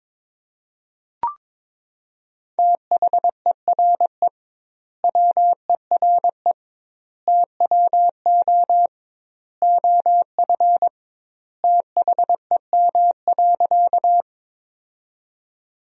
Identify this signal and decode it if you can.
Morse code, heard as a keyed tone, about 22 wpm, 700 Hz. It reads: THERE WERE TWO OF THEM.
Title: 2x Word-Spacing - Sentences from Top 100 Words - 22wpm